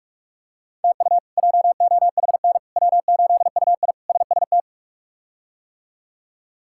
Morse code, heard as a keyed tone, about 41 wpm, 700 Hz. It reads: TU JOHN W8UI SST
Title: K1USN SST QSO Practice - 45wpm